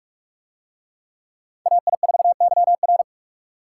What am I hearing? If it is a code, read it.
Morse code, heard as a keyed tone, about 43 wpm, 700 Hz. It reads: AI4YR